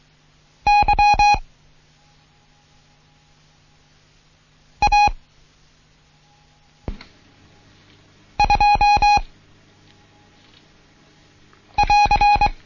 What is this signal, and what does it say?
Morse code, heard as a keyed tone, about 23 wpm, 830 Hz. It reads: Y A 2 <AR>
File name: Morsen / Kurs A 16